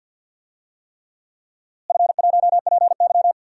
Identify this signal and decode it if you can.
Morse code, heard as a keyed tone, about 47 wpm, 700 Hz. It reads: F1PY
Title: DXCC-100 Call Signs - 4 Characters - Repeated 3x - 50wpm